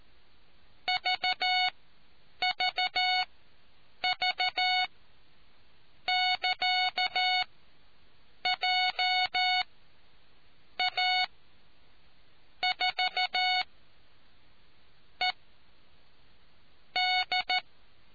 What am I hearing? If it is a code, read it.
Morse code, heard as a keyed tone, about 14 wpm, spaced as for 6 wpm, 740 Hz. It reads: VVV<KA>JA4ED